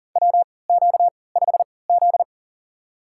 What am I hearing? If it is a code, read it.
Morse code, heard as a keyed tone, about 40 wpm, spaced as for 23 wpm, 700 Hz. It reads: WQ5Z
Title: USA Call Signs - 3x Character Spacing - 40wpm